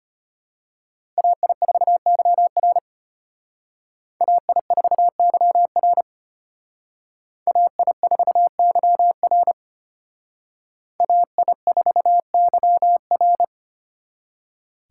Morse code, heard as a keyed tone, about 31 wpm, 700 Hz. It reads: AI4YR AI4YR AI4YR AI4YR
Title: Speed Racing - USA Call Signs 25wpm